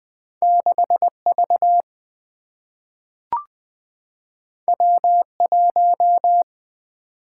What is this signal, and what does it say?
Morse code, heard as a keyed tone, about 20 wpm, 700 Hz. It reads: 6 V W 1
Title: Single-Letter Single-Number 20wpm